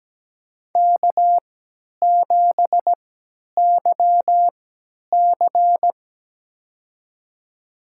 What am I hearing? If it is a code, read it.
Morse code, heard as a keyed tone, about 17 wpm, spaced as for 10 wpm, 700 Hz. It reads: K7YC
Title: USA Call Signs - 4 Characters - 3x Character Spacing - 17wpm